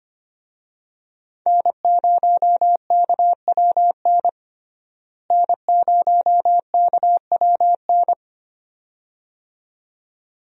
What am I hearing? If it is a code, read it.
Morse code, heard as a keyed tone, about 25 wpm, 700 Hz. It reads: N0KWN N0KWN
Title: USA Call Signs - 5 Characters - Repeated 2x - 25wpm